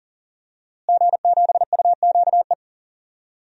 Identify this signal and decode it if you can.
Morse code, heard as a keyed tone, about 40 wpm, 700 Hz. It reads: G7UQE
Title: DXCC-100 Call Signs - 5 Characters - Repeated 2x - 40wpm